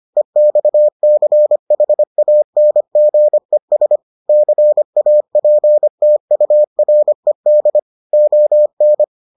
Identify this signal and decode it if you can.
Morse code, heard as a keyed tone, about 25 wpm, 600 Hz. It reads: EXCHANGES CAPTURED ON